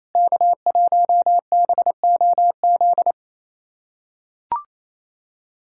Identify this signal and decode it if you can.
Morse code, heard as a keyed tone, about 28 wpm, 700 Hz. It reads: K1BOZ
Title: USA Callsign DE Callsign - 5 Characters - 28wpm